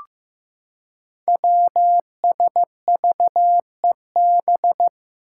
Morse code, heard as a keyed tone, about 15 wpm, 700 Hz. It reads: WSVEB
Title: Five Challenging Letter-Number Combinations 15wpm